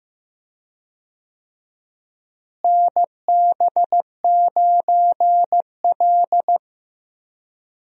Morse code, heard as a keyed tone, about 15 wpm, 700 Hz. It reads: NB9L
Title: USA Call Signs - 4 Characters - Repeated 2x - 15wpm